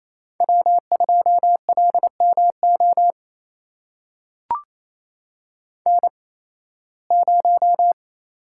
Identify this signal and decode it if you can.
Morse code, heard as a keyed tone, about 28 wpm, spaced as for 18 wpm, 700 Hz. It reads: W2LMO N0
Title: USA Call Signs - 5 Characters - 8x Character Spacing - 28wpm